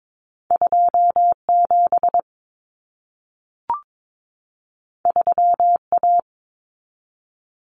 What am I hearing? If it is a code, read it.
Morse code, heard as a keyed tone, about 22 wpm, 700 Hz. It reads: 27 3A
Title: Common Cut Numbers - 2-Digit 22wpm